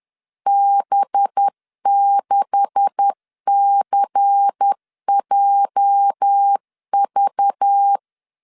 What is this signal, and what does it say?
Morse code, heard as a keyed tone, about 10 wpm, 800 Hz. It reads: B6CJV
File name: LEZIONI / 9 286 10wpm 2735